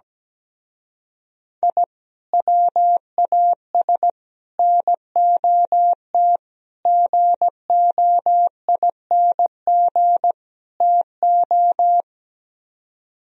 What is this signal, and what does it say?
Morse code, heard as a keyed tone, about 17 wpm, 700 Hz. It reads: I WAS NOT GOING TO
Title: Speed-Racing - Sets of 5 Words 17wpm